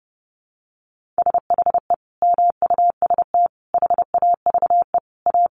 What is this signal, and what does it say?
Morse code, heard as a keyed tone, about 30 wpm, 700 Hz. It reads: SHE MUST HAVE A
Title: Sentences from Top 300 Words 30wpm